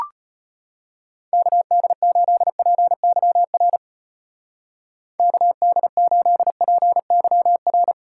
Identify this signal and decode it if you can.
Morse code, heard as a keyed tone, about 32 wpm, 700 Hz. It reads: KD8PYR KD8PYR